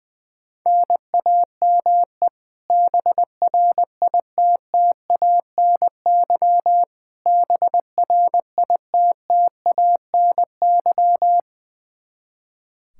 Morse code, heard as a keyed tone, about 20 wpm, 700 Hz. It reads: NAME BRITTANY BRITTANY
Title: Practice QSO - Exchange 1 - Beginning - 20wpm